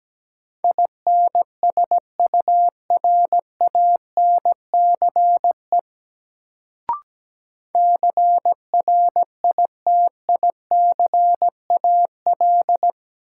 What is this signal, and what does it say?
Morse code, heard as a keyed tone, about 17 wpm, 700 Hz. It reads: INSURANCE CRITICAL
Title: Sentences from Top 1100 Words - New Word Drill - 17wpm